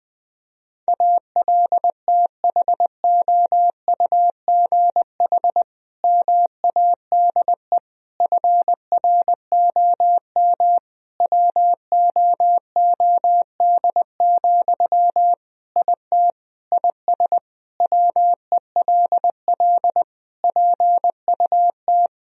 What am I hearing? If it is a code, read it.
Morse code, heard as a keyed tone, about 20 wpm, 700 Hz. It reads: ALTHOUGH MADE FROM WOOD, IT IS WELL PUT
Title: Sentences from Top 1000 Words - Review Part 3 - 20wpm